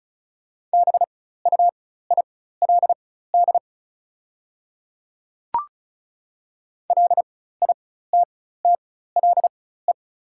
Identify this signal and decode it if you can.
Morse code, heard as a keyed tone, about 34 wpm, spaced as for 16 wpm, 690 Hz. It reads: BUILD LITTLE
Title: 4x Character Spacing - Top 200 Words - 35wpm